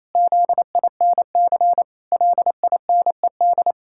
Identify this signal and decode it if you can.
Morse code, heard as a keyed tone, about 28 wpm, 700 Hz. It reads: ZINC LINED
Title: Pangram Challenge v2 - 28wpm